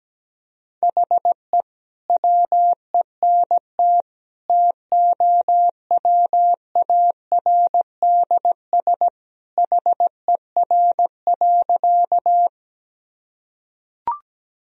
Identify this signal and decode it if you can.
Morse code, heard as a keyed tone, about 17 wpm, 700 Hz. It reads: HE WENT TOWARDS HER.
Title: Sentences from Top 1000 Words - Review Part 2 - 17wpm